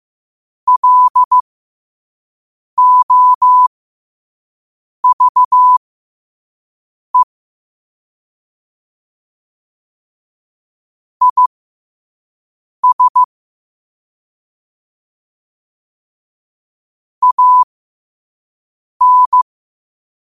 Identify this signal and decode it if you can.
Morse code, heard as a keyed tone, about 15 wpm, spaced as for 5 wpm, 1000 Hz. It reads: LOVE IS AN
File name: qotdfiles / 5wpm